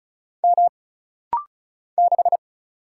Morse code, heard as a keyed tone, about 36 wpm, 700 Hz. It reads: M 6
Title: Single Letter-Number - Rapid-Fire 35wpm